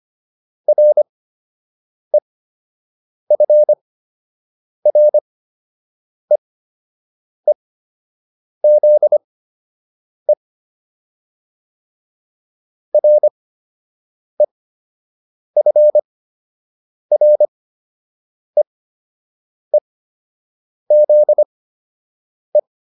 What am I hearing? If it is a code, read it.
Morse code, heard as a keyed tone, about 25 wpm, spaced as for 7 wpm, 600 Hz. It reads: REFREEZE REFREEZE